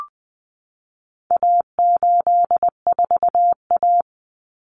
Morse code, heard as a keyed tone, about 20 wpm, 700 Hz. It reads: A84A
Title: Common Cut Numbers - 4-Digit 20wpm